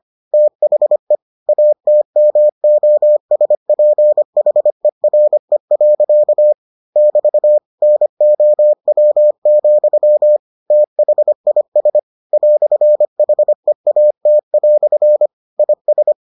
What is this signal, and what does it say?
Morse code, heard as a keyed tone, about 25 wpm, 600 Hz. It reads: THE ATMOSPHERE. <BT> NOW, THIS "HEAT" IS